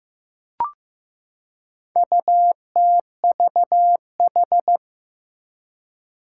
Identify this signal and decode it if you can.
Morse code, heard as a keyed tone, about 15 wpm, 700 Hz. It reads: UTVH
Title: Four Challenging Letter-Number Combinations 15wpm